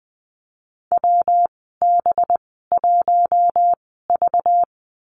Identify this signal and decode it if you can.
Morse code, heard as a keyed tone, about 20 wpm, spaced as for 14 wpm, 700 Hz. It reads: WB1V
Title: 193 Lesson 31g - Letter v - Call Signs 20wpm